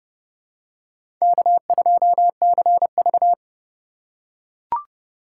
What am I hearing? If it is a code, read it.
Morse code, heard as a keyed tone, about 28 wpm, 700 Hz. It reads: K2CV